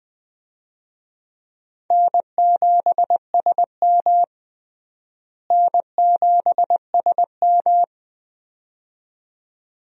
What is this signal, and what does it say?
Morse code, heard as a keyed tone, about 21 wpm, 700 Hz. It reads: N7SM N7SM
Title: USA Call Signs - 4 Characters - Repeated 2x - 20wpm